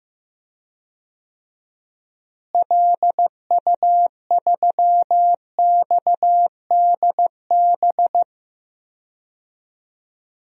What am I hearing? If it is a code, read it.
Morse code, heard as a keyed tone, about 15 wpm, 700 Hz. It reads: LU3XDB
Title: DXCC-100 Call Signs - 6 Characters - Repeated 3x - 15wpm